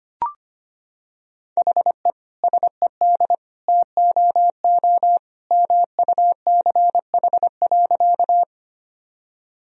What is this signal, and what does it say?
Morse code, heard as a keyed tone, about 25 wpm, 700 Hz. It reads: HE SED TOO MUCH.
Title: Sentences from Top 200 Words and Abbreviations 25wpm